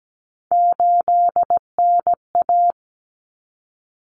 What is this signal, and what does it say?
Morse code, heard as a keyed tone, about 17 wpm, 700 Hz. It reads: 8NA
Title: Common Cut Numbers - 3-Digit 17wpm